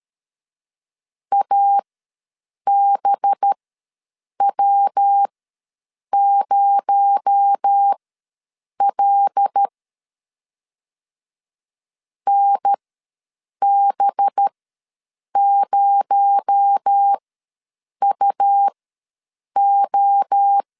Morse code, heard as a keyed tone, about 13 wpm, spaced as for 7 wpm, 800 Hz. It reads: ABW0L NB0UO
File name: LEZIONI / 7 GF9Z / 7 GF ws 12wpm 3140